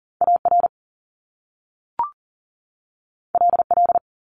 Morse code, heard as a keyed tone, about 42 wpm, 700 Hz. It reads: AR LL
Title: Top Two Letter Combinations in Words 40wpm